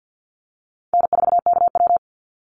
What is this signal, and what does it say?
Morse code, heard as a keyed tone, about 48 wpm, 700 Hz. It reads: N4KW
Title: Easy Call Signs Repeated 2x - 50wpm